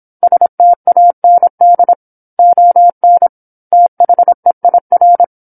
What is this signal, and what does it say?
Morse code, heard as a keyed tone, about 27 wpm, 700 Hz. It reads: STAND ON THEIR
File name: 08-Chapter-08 26 WPM